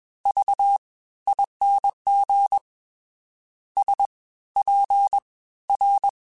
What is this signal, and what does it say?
Morse code, heard as a keyed tone, about 21 wpm, spaced as for 15 wpm, 800 Hz. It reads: VING SPR